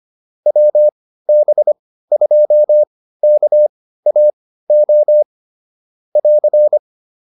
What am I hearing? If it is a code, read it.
Morse code, heard as a keyed tone, about 25 wpm, spaced as for 15 wpm, 600 Hz. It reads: WB2KAO <AR>